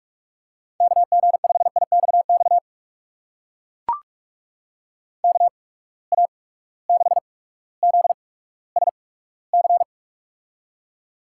Morse code, heard as a keyed tone, about 44 wpm, spaced as for 19 wpm, 700 Hz. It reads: KG5IXX KA6ZSC
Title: USA Call Signs - 6 Characters - 8x Character Spacing - 45wpm